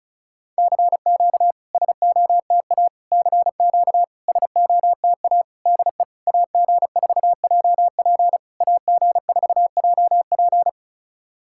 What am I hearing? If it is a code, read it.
Morse code, heard as a keyed tone, about 36 wpm, 660 Hz. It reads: CQ SOTA CQ SOTA DE AG4JP AG4JP
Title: SOTA QSO Practice 35wpm